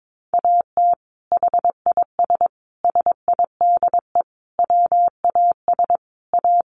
Morse code, heard as a keyed tone, about 22 wpm, 700 Hz. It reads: AT HIS SIDE WAS A
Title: Sentences from Top 500 Words 22wpm